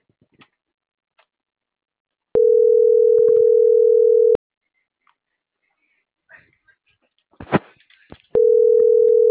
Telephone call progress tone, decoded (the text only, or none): ringback tone